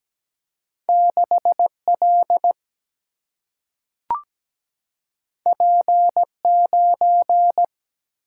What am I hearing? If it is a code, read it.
Morse code, heard as a keyed tone, about 17 wpm, 700 Hz. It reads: 6 L P 9
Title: Single-Letter Single-Number 17wpm